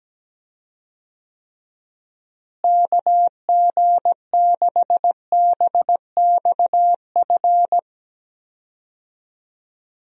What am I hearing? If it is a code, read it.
Morse code, heard as a keyed tone, about 17 wpm, 700 Hz. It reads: KG6BXF
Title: USA Call Signs - 6 Characters - Repeated 3x - 17wpm